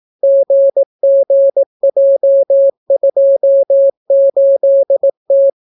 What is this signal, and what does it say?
Morse code, heard as a keyed tone, about 18 wpm, 550 Hz. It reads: GGJ28T